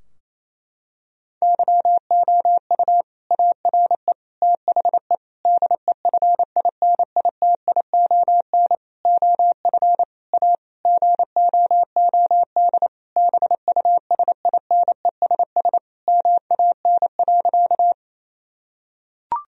Morse code, heard as a keyed tone, about 27 wpm, 700 Hz. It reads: YOU ARE THE DEFINITION OF A GOOD BUSINESS MAN.
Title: Sentences from Top 1000 Words - Review Part 4 - 28wpm